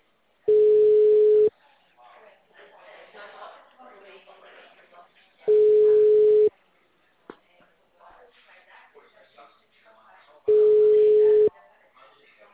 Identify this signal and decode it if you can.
Ringback tone (Europe, 425 Hz), with speech.
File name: out- -2001- - - .46